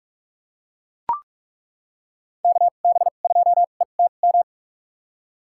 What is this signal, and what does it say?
Morse code, heard as a keyed tone, about 41 wpm, spaced as for 31 wpm, 700 Hz. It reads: KB2ETM
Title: USA Call Signs - 6 Characters - 2x Character Spacing - 45wpm